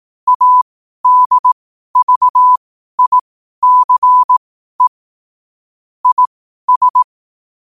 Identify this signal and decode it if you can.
Morse code, heard as a keyed tone, about 18 wpm, spaced as for 12 wpm, 1000 Hz. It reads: ADVICE IS
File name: qotdfiles / 12wpm